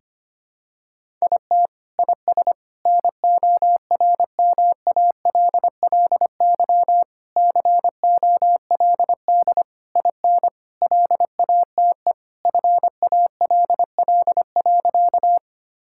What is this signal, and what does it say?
Morse code, heard as a keyed tone, about 25 wpm, 700 Hz. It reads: IT IS NORMALLY COLD IN LATE FALL.
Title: Sentences from Top 1000 Words - Review Part 6 - 25wpm